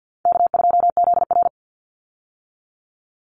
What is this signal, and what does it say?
Morse code, heard as a keyed tone, about 50 wpm, 700 Hz. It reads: K2ZR
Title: Call Signs - 50wpm